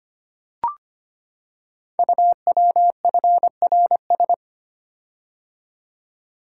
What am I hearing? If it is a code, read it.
Morse code, heard as a keyed tone, about 25 wpm, 700 Hz. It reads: UWFRS
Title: Code Groups - 5 Letters - 25wpm